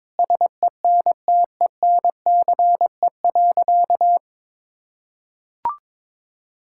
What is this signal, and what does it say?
Morse code, heard as a keyed tone, about 22 wpm, 700 Hz. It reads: SENTENCE.
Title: Sentences from Top 1600 Words 22wpm